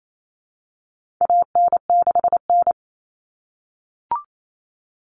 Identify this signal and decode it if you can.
Morse code, heard as a keyed tone, about 28 wpm, 700 Hz. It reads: AN6N
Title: Common Cut Numbers - 4-Digit 28wpm